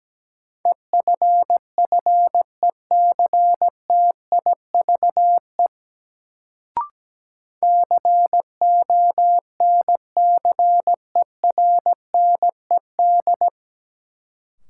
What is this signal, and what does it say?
Morse code, heard as a keyed tone, about 17 wpm, 700 Hz. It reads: EFFECTIVE CONCERNED
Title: Sentences from Top 1000 Words - Review Part 6 - New Word Drill - 17wpm